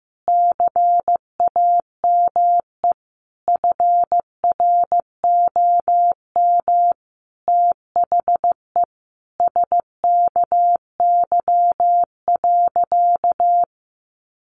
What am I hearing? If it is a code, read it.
Morse code, heard as a keyed tone, about 15 wpm, 700 Hz. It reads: CAME FROM THE SKY.
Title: Sentences from Top 1200 Words 15wpm